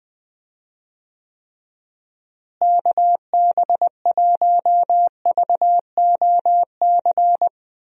Morse code, heard as a keyed tone, about 20 wpm, 700 Hz. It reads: KB1VOC
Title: USA Call Signs - 6 Characters - Repeated 2x - 20wpm